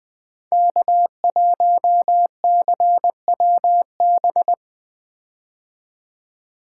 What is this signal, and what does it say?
Morse code, heard as a keyed tone, about 20 wpm, 700 Hz. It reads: K1CWB
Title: 2023 POTA Hunter Call Signs - 20wpm